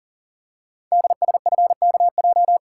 Morse code, heard as a keyed tone, about 42 wpm, 700 Hz. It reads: DSFKJ
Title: Code Groups - 5 Letters - 40wpm